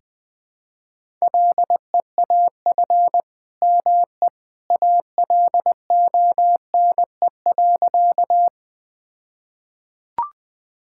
Morse code, heard as a keyed tone, about 21 wpm, 700 Hz. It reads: LEAF ME ALONE.
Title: All-Age Appropriate Jokes - 20wpm